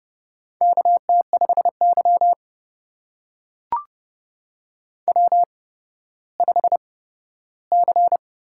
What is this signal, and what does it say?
Morse code, heard as a keyed tone, about 30 wpm, spaced as for 15 wpm, 700 Hz. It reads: KT5Y W5C
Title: USA Call Signs - 8x Character Spacing - 30wpm